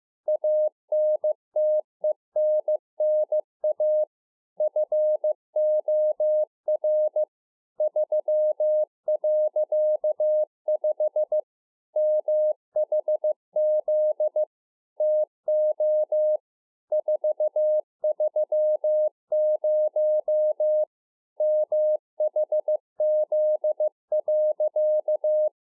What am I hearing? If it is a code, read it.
Morse code, heard as a keyed tone, about 15 wpm, 610 Hz. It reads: ANTENNA FOR 3.5 MHZ TO 430 MHZ.